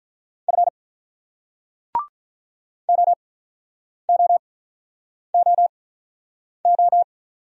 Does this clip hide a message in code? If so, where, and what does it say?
Yes: Morse code from 0.5 to 7.0 s, heard as a keyed tone, about 47 wpm, spaced as for 9 wpm, 690 Hz. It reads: F OOOO